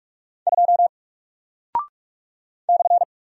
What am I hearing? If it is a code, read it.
Morse code, heard as a keyed tone, about 48 wpm, 700 Hz. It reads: 2 /
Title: Single Character (Letter:Number:Punctuation:Prosign) Alt 45wpm